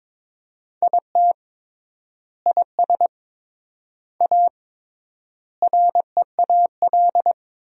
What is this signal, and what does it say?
Morse code, heard as a keyed tone, about 22 wpm, 700 Hz. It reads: IT IS A REAL
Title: 3x Word-Spacing - Sentences from Top 500 Words - 22wpm